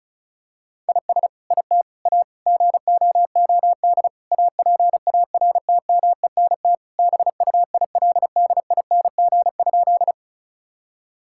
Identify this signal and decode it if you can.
Morse code, heard as a keyed tone, about 36 wpm, 700 Hz. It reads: IS IT A GOOD APARTMENT BUILDING?
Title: Sentences from Top 1000 Words - Review Part 6 - 35wpm